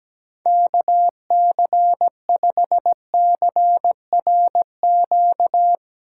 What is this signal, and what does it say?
Morse code, heard as a keyed tone, about 17 wpm, 700 Hz. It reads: KC5CRQ
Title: USA Call Signs - 6 Characters - 8x Character Spacing - 17wpm